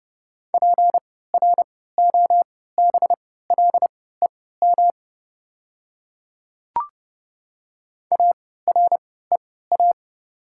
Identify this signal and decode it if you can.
Morse code, heard as a keyed tone, about 30 wpm, spaced as for 17 wpm, 690 Hz. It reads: PROBLEM AREA
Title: 3x Character Spacing - Top 500 Words - 30wpm